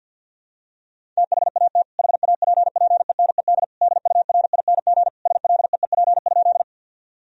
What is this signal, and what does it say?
Morse code, heard as a keyed tone, about 50 wpm, 700 Hz. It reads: THAT HAPPENED DURING SLEEP?
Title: Sentences from Top 1000 Words - Review Part 2 - 50wpm